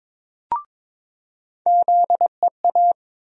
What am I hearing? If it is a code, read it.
Morse code, heard as a keyed tone, about 22 wpm, 700 Hz. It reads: ZEA
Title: USA Call Sign Postfixes - 3 Letters 22wpm